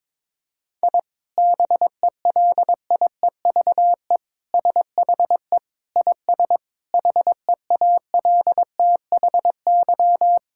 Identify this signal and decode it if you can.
Morse code, heard as a keyed tone, about 21 wpm, 700 Hz. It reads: I BELIEVE SHE IS HEALTHY
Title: Sentences from Top 1000 Words - Review Part 5 - 22wpm